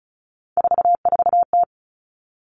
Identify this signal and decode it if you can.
Morse code, heard as a keyed tone, about 36 wpm, 700 Hz. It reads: 44T
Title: Three Challenging Letter-Number Combinations 35wpm